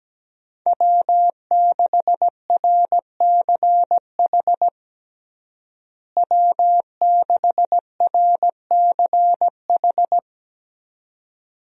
Morse code, heard as a keyed tone, about 17 wpm, 700 Hz. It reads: W6RCH W6RCH
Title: USA Call Signs - 5 Characters - Repeated 2x - 17wpm